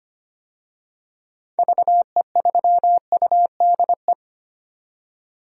Morse code, heard as a keyed tone, about 25 wpm, 700 Hz. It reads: VE3UDE
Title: DXCC-100 Call Signs - 6 Characters - 25wpm